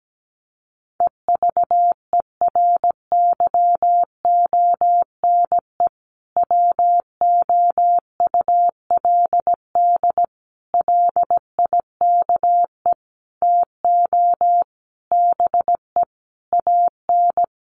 Morse code, heard as a keyed tone, about 17 wpm, 700 Hz. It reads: EVERYONE WOULD LIKE TO BE AN
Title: Sentences from Top 1500 Words 17wpm